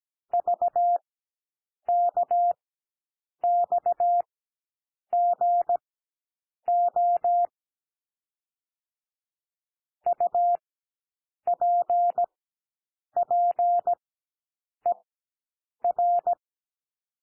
Morse code, heard as a keyed tone, about 17 wpm, spaced as for 7 wpm, 700 Hz. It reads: VKXGO UPPER
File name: SOUND LETTERS 17 wpm